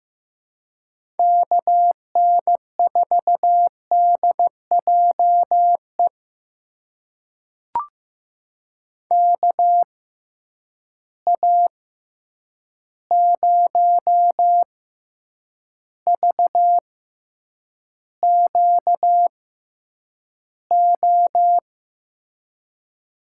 Morse code, heard as a keyed tone, about 15 wpm, spaced as for 8 wpm, 700 Hz. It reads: KN4DJE KA0VQO
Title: USA Call Signs - 6 Characters - 6x Character Spacing - 15wpm